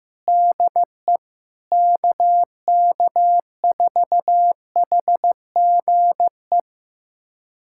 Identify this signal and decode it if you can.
Morse code, heard as a keyed tone, about 16 wpm, 700 Hz. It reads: DE KK4HGE
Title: USA Callsign DE Callsign - 6 Characters - 15wpm